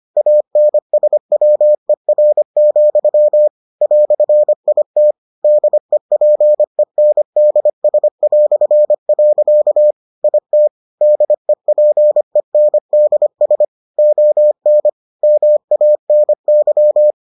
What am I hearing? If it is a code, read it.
Morse code, heard as a keyed tone, about 25 wpm, 600 Hz. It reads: ANSWER, "IT DEPENDS". IT DEPENDS ON MANY